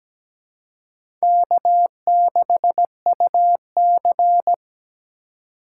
Morse code, heard as a keyed tone, about 17 wpm, 700 Hz. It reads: K6UC
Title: USA Call Signs - 4 Characters - 4x Character Spacing - 17wpm